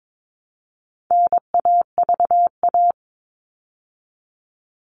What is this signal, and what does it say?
Morse code, heard as a keyed tone, about 22 wpm, 700 Hz. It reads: NAVA
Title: All Cut Numbers - 4-Digit 22wpm